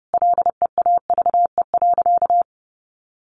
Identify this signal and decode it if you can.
Morse code, heard as a keyed tone, about 30 wpm, 700 Hz. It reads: LEAVE.
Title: Sentences from Top 1500 Words 30wpm